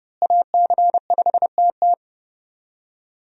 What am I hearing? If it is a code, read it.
Morse code, heard as a keyed tone, about 31 wpm, 700 Hz. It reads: AC5TT
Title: USA Call Signs - 5 Characters - Repeated 3x - 30wpm